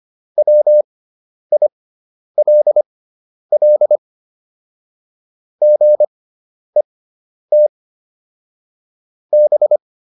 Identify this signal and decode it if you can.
Morse code, heard as a keyed tone, about 25 wpm, spaced as for 10 wpm, 600 Hz. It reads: WILL GET B